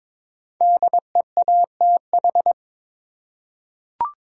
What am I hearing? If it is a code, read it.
Morse code, heard as a keyed tone, about 22 wpm, 700 Hz. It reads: DEATH
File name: 5 Letter Words - First 3 Letters Given - 22wpm